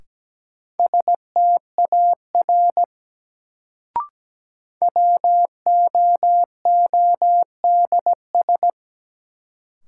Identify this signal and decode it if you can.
Morse code, heard as a keyed tone, about 17 wpm, 700 Hz. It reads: STAR WOODS
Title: Sentences from Top 1000 Words - Review Part 2 - New Word Drill - 17wpm